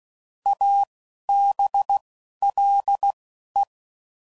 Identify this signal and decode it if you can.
Morse code, heard as a keyed tone, about 16 wpm, spaced as for 11 wpm, 800 Hz. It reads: ABLE
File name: Prefixes / By First Letter / As